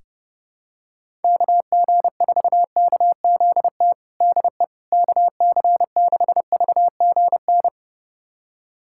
Morse code, heard as a keyed tone, about 30 wpm, 700 Hz. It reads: KG4KZT DE KC6VGN
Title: USA Callsign DE Callsign 30wpm